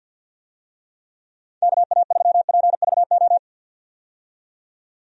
Morse code, heard as a keyed tone, about 51 wpm, 700 Hz. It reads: KA3PVO
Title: USA Call Signs - 6 Characters - 50wpm